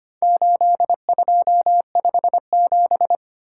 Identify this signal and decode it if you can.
Morse code, heard as a keyed tone, about 25 wpm, 700 Hz. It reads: 8257
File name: practice / 25wpm / Four Letter-Number Combinations - 25wpm